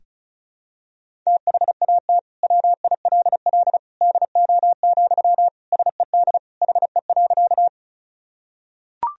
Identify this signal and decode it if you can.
Morse code, heard as a keyed tone, about 34 wpm, 700 Hz. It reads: THAT WILL DO, SED HE.
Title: Sentences from Top 100 Words and Abbreviations 35wpm